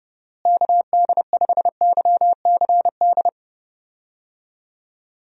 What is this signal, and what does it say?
Morse code, heard as a keyed tone, about 32 wpm, 700 Hz. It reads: KD5YCD
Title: USA Call Signs - 6 Characters - Repeated 2x - 30wpm